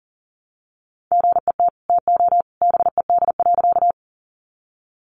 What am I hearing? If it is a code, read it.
Morse code, heard as a keyed tone, about 40 wpm, 700 Hz. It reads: GET TO BED.
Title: Sentences from Top 500 Words 40wpm